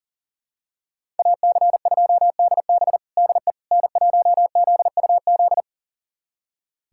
Morse code, heard as a keyed tone, about 37 wpm, 700 Hz. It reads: AC2DB DE N1ZUZ
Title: USA Callsign DE Callsign - 5 Characters - 40wpm